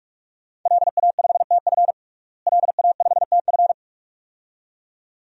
Morse code, heard as a keyed tone, about 46 wpm, 700 Hz. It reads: LA5TF LA5TF
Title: DXCC-100 Call Signs - 5 Characters - Repeated 2x - 45wpm